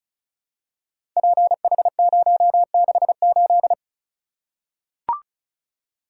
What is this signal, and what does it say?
Morse code, heard as a keyed tone, about 36 wpm, 700 Hz. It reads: PH068
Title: Code Groups - 5 Letter:Numbers - 35wpm